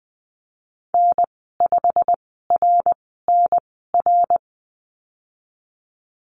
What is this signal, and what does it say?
Morse code, heard as a keyed tone, about 20 wpm, spaced as for 15 wpm, 700 Hz. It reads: N5RNR
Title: 179 Lesson 29g - Slash - Call Signs 20wpm